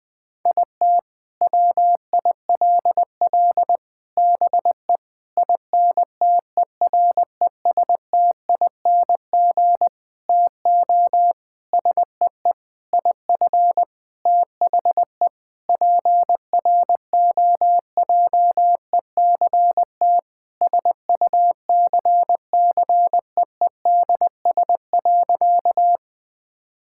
Morse code, heard as a keyed tone, about 20 wpm, 700 Hz. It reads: IT WILL BE INTERESTING TO SEE IF THE PROJECT SUCCEEDS.
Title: Sentences with 8-Word Phrases - 20wpm